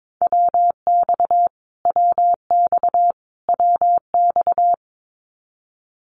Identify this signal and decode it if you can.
Morse code, heard as a keyed tone, about 22 wpm, 700 Hz. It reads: WX WX WX
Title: Abbreviations Repeated 3x - 22wpm